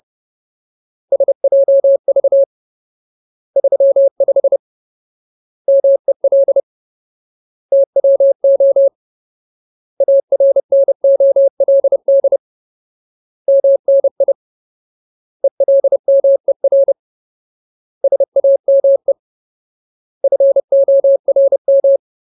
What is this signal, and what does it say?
Morse code, heard as a keyed tone, about 30 wpm, 550 Hz. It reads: SJV 35 MEL TWO ARNOLD MNI ELMER SAME FORM
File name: Ultimate Word List / 30WPM Ultimate